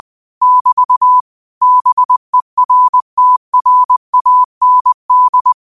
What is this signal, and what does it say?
Morse code, heard as a keyed tone, about 20 wpm, 1000 Hz. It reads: <BT> BERTRAND
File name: qotdfiles / 20wpm